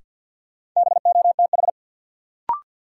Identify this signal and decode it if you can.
Morse code, heard as a keyed tone, about 52 wpm, 700 Hz. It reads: BOTH
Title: Sets of 4 Words - New Word Drill - 50wpm